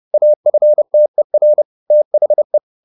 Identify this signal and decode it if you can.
Morse code, heard as a keyed tone, about 30 wpm, 600 Hz. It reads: AFTER THE